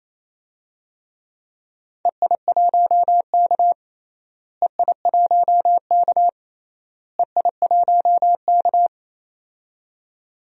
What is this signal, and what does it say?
Morse code, heard as a keyed tone, about 28 wpm, 700 Hz. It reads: EI1K EI1K EI1K
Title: DXCC-100 Call Signs - 4 Characters - Repeated 3x - 28wpm